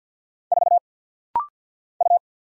Morse code, heard as a keyed tone, about 50 wpm, 700 Hz. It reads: 4 U
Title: Single Letter-Number Alt - Rapid-Fire 50wpm